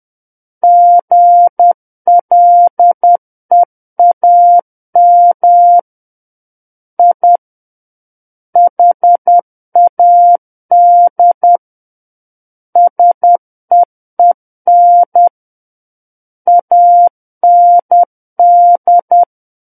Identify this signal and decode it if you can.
Morse code, heard as a keyed tone, about 10 wpm, 700 Hz. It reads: GLEAM I HAD SEEN AND
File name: Book One / 01-Chapter 1 - 10 WPM